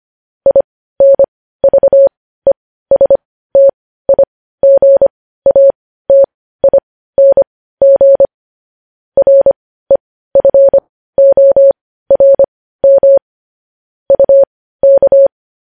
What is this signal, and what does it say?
Morse code, heard as a keyed tone, about 25 wpm, spaced as for 15 wpm, 550 Hz. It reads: INVESTIGATING REFORM UK
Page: news headlines in morse code at 15 wpm